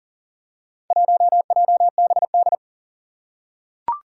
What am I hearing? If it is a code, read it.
Morse code, heard as a keyed tone, about 40 wpm, 690 Hz. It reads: 1JBD
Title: Four Challenging Letter-Number Combinations 40wpm